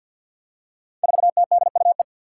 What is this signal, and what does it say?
Morse code, heard as a keyed tone, about 47 wpm, 700 Hz. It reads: 4TDUE
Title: Five Challenging Letter-Number Combinations 50wpm